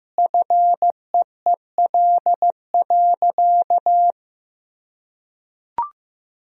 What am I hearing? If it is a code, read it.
Morse code, heard as a keyed tone, about 15 wpm, 700 Hz. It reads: FEEL.
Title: Sentences from Top 1000 Words - Review Part 1 - 15wpm